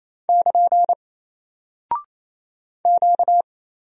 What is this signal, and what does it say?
Morse code, heard as a keyed tone, about 28 wpm, 700 Hz. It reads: <KN> Q
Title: Single Character (Letter:Number:Punctuation:Prosign) Alt 28wpm